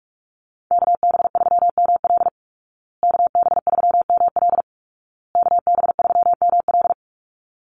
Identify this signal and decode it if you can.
Morse code, heard as a keyed tone, about 44 wpm, 700 Hz. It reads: KB3ML KB3ML KB3ML
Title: Call Signs Repeated 2x - 45wpm